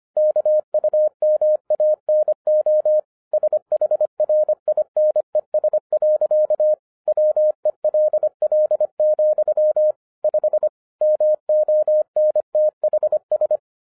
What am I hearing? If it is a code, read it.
Morse code, heard as a keyed tone, about 24 wpm, 610 Hz. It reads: KUMANO SHRINES. WELL, 5 MONTHS
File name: 2011-10-mp3 / n04-JUST MOVED 25wpm